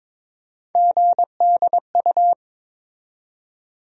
Morse code, heard as a keyed tone, about 22 wpm, 700 Hz. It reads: GDU
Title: USA Call Sign Postfixes - 3 Letters 22wpm